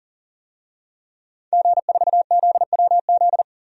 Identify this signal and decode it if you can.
Morse code, heard as a keyed tone, about 40 wpm, 700 Hz. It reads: G4ZWZ
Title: DXCC-100 Call Signs - 5 Characters - 40wpm